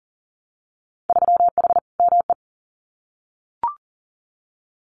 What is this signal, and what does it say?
Morse code, heard as a keyed tone, about 38 wpm, 700 Hz. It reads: 3H ME
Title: Winter Field Day Exchanges Encore 40wpm